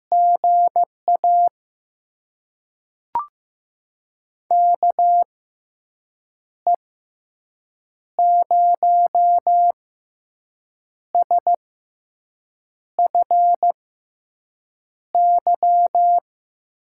Morse code, heard as a keyed tone, about 15 wpm, spaced as for 6 wpm, 700 Hz. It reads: GA KE0SFY
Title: USA Call Signs - 6 Characters - 6x Character Spacing - 15wpm